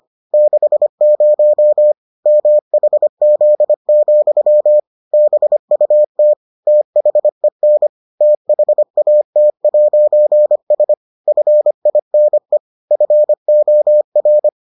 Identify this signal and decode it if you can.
Morse code, heard as a keyed tone, about 25 wpm, 600 Hz. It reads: 60 MHZ, BUT THEN THAT'S FINE FOR